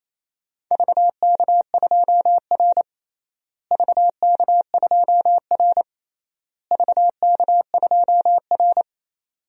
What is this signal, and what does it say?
Morse code, heard as a keyed tone, about 28 wpm, 700 Hz. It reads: VK2R VK2R VK2R
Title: DXCC-100 Call Signs - 4 Characters - Repeated 3x - 28wpm